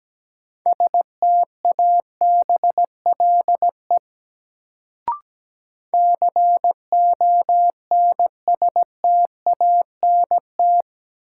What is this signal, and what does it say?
Morse code, heard as a keyed tone, about 17 wpm, 700 Hz. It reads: STABLE CONSTANT
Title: Sentences from Top 1500 Words - New Word Drill - 17wpm